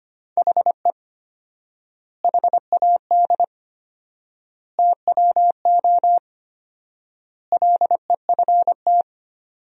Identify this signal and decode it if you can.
Morse code, heard as a keyed tone, about 25 wpm, 700 Hz. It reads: HE HAD TWO LEFT
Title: 4x Word-Spacing - Sentences from Top 300 Words - 25wpm